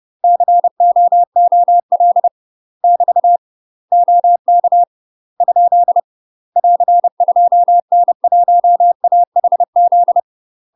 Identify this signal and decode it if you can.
Morse code, heard as a keyed tone, about 30 wpm, 700 Hz. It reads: COOL <BT> OK ? <AR> 2N1AHZ
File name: UK MASTER QSOs TRAINING / 30WPM / UK MASTER QSOs 0044